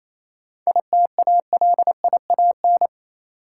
Morse code, heard as a keyed tone, about 28 wpm, 690 Hz. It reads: ITALIAN